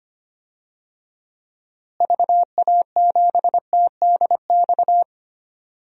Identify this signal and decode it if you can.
Morse code, heard as a keyed tone, about 25 wpm, 700 Hz. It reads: VA7TDX
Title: DXCC-100 Call Signs - 6 Characters - Repeated 2x - 25wpm